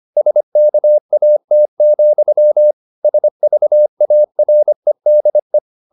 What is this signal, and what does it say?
Morse code, heard as a keyed tone, about 25 wpm, 600 Hz. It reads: SKAT, SVAREDE
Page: Ferie 25 wpm